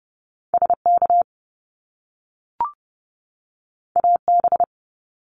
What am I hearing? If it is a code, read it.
Morse code, heard as a keyed tone, about 30 wpm, 700 Hz. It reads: SK AB
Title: Canadian Provinces and Territories 30wpm